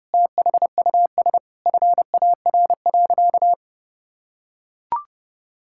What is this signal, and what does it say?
Morse code, heard as a keyed tone, about 30 wpm, 700 Hz. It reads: THUS FAR.
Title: Sentences from Top 1000 Words - Review Part 5 - 30wpm